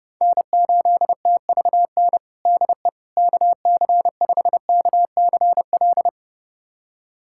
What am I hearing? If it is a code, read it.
Morse code, heard as a keyed tone, about 32 wpm, 700 Hz. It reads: N8TVN DE KC5KCL
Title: USA Callsign DE Callsign 30wpm